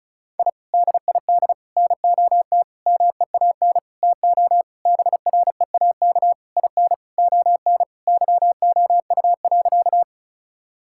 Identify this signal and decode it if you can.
Morse code, heard as a keyed tone, about 36 wpm, 700 Hz. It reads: I DID NOT MEAN TO BREAK IN ON YOU.
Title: Sentences from Top 1000 Words - Review Part 6 - 35wpm